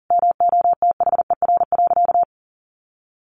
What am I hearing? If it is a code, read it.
Morse code, heard as a keyed tone, about 40 wpm, 700 Hz. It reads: MOTHER.
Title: Sentences from Top 700 Words 40wpm